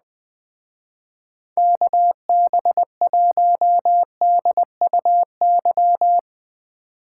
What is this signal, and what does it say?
Morse code, heard as a keyed tone, about 20 wpm, 700 Hz. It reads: KB1DUY
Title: Speed Racing - USA 2x3 Call Signs 17wpm